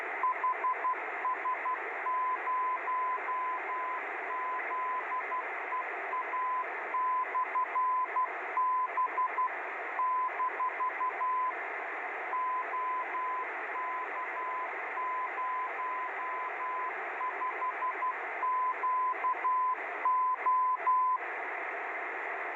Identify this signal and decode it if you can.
Morse code, heard as a keyed tone, about 12 wpm, 1010 Hz, with radio noise. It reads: HS0ZEA/B - OK05QO